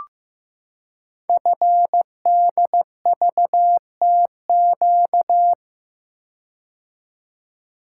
Morse code, heard as a keyed tone, about 15 wpm, 700 Hz. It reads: FDVTQ
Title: Code Groups - 5 Letters - 15wpm